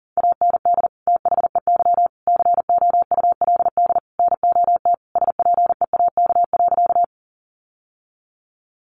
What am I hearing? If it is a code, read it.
Morse code, heard as a keyed tone, about 40 wpm, 700 Hz. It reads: AND THEY COULD NOT SPEAK.
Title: Sentences from Top 1100 Words 40wpm